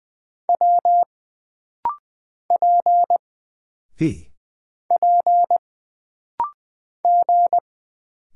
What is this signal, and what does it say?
Morse code, heard as a keyed tone, about 20 wpm, 700 Hz. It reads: W P P G
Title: Single Character (Letter:Number:Punctuation:Prosign) - Rapid-Fire - 20wpm